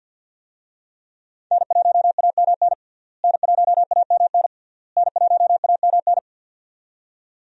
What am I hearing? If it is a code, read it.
Morse code, heard as a keyed tone, about 53 wpm, 700 Hz. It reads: N1AMN N1AMN N1AMN
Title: USA Call Signs - 5 Characters - Repeated 3x - 50wpm